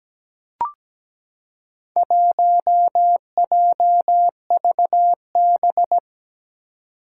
Morse code, heard as a keyed tone, about 17 wpm, 700 Hz. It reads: 1JVB
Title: Four Challenging Letter-Number Combinations 17wpm